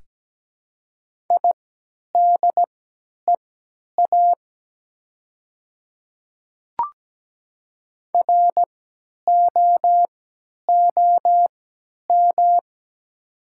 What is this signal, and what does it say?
Morse code, heard as a keyed tone, about 17 wpm, spaced as for 10 wpm, 700 Hz. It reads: IDEA ROOM
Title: 3x Character Spacing - Top 300 Words - 17wpm